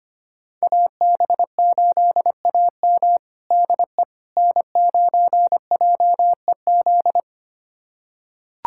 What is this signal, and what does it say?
Morse code, heard as a keyed tone, about 25 wpm, 700 Hz. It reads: AB8AM DE N9JEZ
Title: USA Callsign DE Callsign - 5 Characters - 25wpm